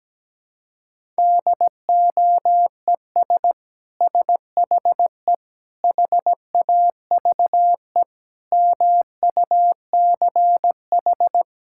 Morse code, heard as a keyed tone, about 17 wpm, 700 Hz. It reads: DOES SHE HAVE MUCH
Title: Sentences from Top 1000 Words - Review Part 6 - 17wpm